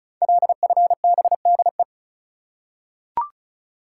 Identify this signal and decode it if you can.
Morse code, heard as a keyed tone, about 36 wpm, 700 Hz. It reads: LFBDE
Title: Code Groups - 5 Letters - 35wpm